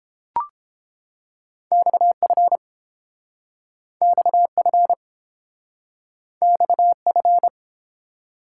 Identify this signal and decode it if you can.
Morse code, heard as a keyed tone, about 30 wpm, 700 Hz. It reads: XF XF XF